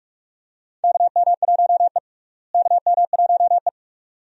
Morse code, heard as a keyed tone, about 42 wpm, 700 Hz. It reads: KM1E KM1E
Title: USA Call Signs - 4 Characters - Repeated 2x - 45wpm